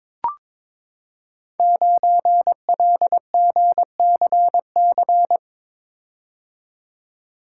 Morse code, heard as a keyed tone, about 22 wpm, 700 Hz. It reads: 9LGCC
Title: Code Groups - 5 Letter:Numbers - 22wpm